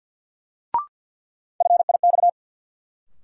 Morse code, heard as a keyed tone, about 47 wpm, 700 Hz. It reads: FIX
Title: Sentences from Top 1400 Words - New Word Drill - 50wpm